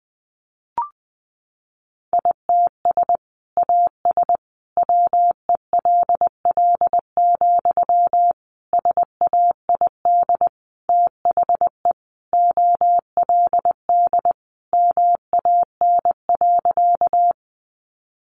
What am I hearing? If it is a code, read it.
Morse code, heard as a keyed tone, about 20 wpm, 700 Hz. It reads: ITS AS WELL, SAID THE OLD MAN.
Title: Sentences from Top 600 Words 20wpm